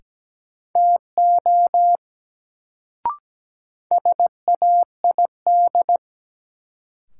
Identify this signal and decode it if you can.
Morse code, heard as a keyed tone, about 17 wpm, 700 Hz. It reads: TO SAID
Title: Sets of 2 Words - New Word Drill - 17wpm